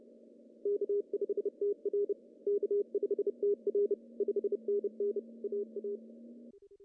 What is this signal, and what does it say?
Morse code, heard as a keyed tone, about 30 wpm, 400 Hz. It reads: K5TR K5TR 5NN AA
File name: CW / cqww-cw-k5tr